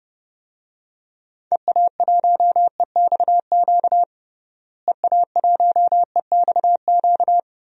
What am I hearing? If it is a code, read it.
Morse code, heard as a keyed tone, about 30 wpm, 700 Hz. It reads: EA1EXQ EA1EXQ
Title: DXCC-100 Call Signs - 6 Characters - Repeated 2x - 30wpm